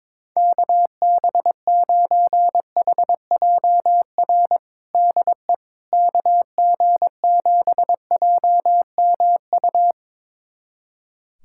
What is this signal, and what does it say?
Morse code, heard as a keyed tone, about 22 wpm, 700 Hz. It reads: KB9HJR DE KG7JMU
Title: USA Callsign DE Callsign - 6 Characters - 22wpm